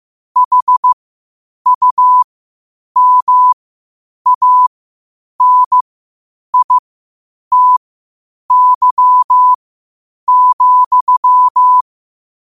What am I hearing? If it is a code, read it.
Morse code, heard as a keyed tone, about 15 wpm, spaced as for 8 wpm, 1000 Hz. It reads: HUMANITY,
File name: qotdfiles / 8wpm